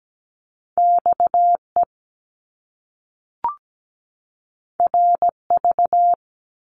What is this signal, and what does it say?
Morse code, heard as a keyed tone, about 17 wpm, 700 Hz. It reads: XE RV
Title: Top Two Letter Combinations in Words 17wpm